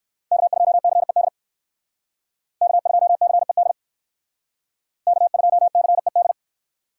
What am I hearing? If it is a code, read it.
Morse code, heard as a keyed tone, about 59 wpm, 700 Hz. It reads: K2CED K2CED K2CED